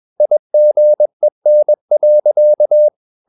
Morse code, heard as a keyed tone, about 21 wpm, 600 Hz. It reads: IGEN.
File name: Ferie QRS 21wpm